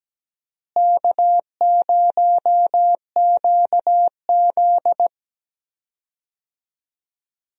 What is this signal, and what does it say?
Morse code, heard as a keyed tone, about 17 wpm, 700 Hz. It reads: K0QZ
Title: USA Call Signs - 4 Characters - Repeated 3x - 17wpm